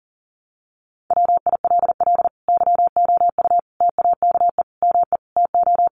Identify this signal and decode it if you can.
Morse code, heard as a keyed tone, about 40 wpm, 700 Hz. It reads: WILL YOU TAKE ME TO
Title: Sentences from Top 200 Words 40wpm